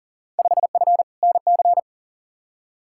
Morse code, heard as a keyed tone, about 40 wpm, 700 Hz. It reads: 5F NC
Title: Field Day Exchanges Encore - 40wpm